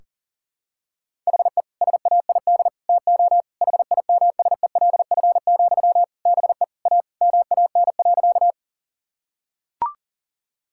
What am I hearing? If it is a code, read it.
Morse code, heard as a keyed tone, about 40 wpm, 700 Hz. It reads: HE SAID TO HIMSELF, BE A MAN.
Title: Sentences from Top 1600 Words 40wpm